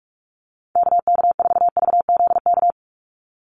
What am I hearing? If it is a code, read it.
Morse code, heard as a keyed tone, about 44 wpm, 700 Hz. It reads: KK4VZK
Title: USA Call Signs 45wpm